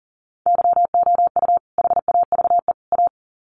Morse code, heard as a keyed tone, about 40 wpm, 700 Hz. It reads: YOU HAVE A
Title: Sets of 3 Words 40wpm